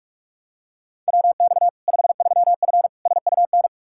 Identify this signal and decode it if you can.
Morse code, heard as a keyed tone, about 46 wpm, 700 Hz. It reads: WX 53F SUN
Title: Practice QSO - Exchange 2 - Info - 45wpm